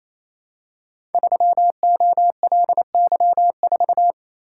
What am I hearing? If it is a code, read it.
Morse code, heard as a keyed tone, about 27 wpm, 700 Hz. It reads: 3OLY4
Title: Code Groups - 5 Letter:Numbers - 28wpm